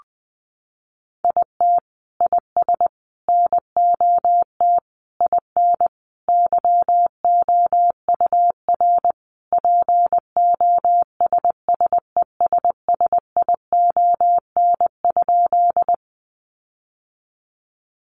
Morse code, heard as a keyed tone, about 20 wpm, 700 Hz. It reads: IT IS NOT IN YOUR POSSESSION?
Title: Sentences from Top 1400 Words 20wpm